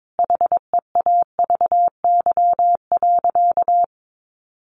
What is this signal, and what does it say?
Morse code, heard as a keyed tone, about 22 wpm, 700 Hz. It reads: HEAVY.
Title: Sentences from Top 1100 Words 22wpm